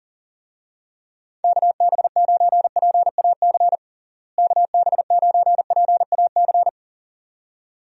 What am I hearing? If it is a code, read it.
Morse code, heard as a keyed tone, about 40 wpm, 700 Hz. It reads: KB9PAC KB9PAC
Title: USA Call Signs - 6 Characters - Repeated 2x - 40wpm